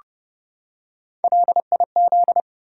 Morse code, heard as a keyed tone, about 30 wpm, 700 Hz. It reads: LIZ
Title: USA Common Names - Max 3 Letters - 30wpm